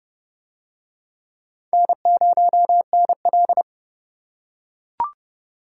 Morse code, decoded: N0NL

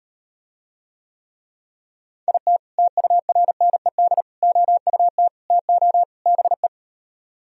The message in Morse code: IT TURNED OUT TO BE

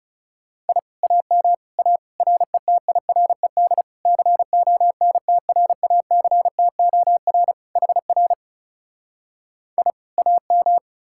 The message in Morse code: I AM A RETIRED CONTRACTOR HR I AM

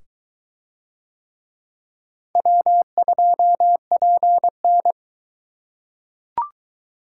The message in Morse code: W2PN

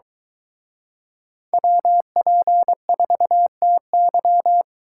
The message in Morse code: WP4TY

